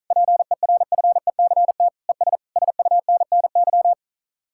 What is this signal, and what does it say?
Morse code, heard as a keyed tone, about 42 wpm, 700 Hz. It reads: PERFECT ES SUNNY
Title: Speed-Racing Common QSO Phrases Encore - 35wpm